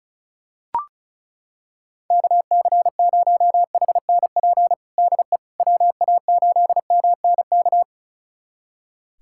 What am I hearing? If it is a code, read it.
Morse code, heard as a keyed tone, about 36 wpm, 700 Hz. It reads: KC0HNP DE WA8MNK
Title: USA Callsign DE Callsign 35wpm